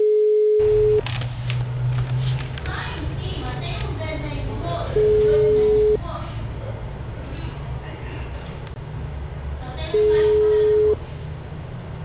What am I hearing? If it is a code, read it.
Ringback tone (Europe, 425 Hz), with speech.
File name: out- -2015- - - .33960